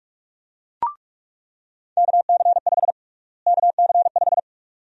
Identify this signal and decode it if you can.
Morse code, heard as a keyed tone, about 44 wpm, 700 Hz. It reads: KC5 KC5
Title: USA Call Sign Prefixes - Repeated 2x - 45wpm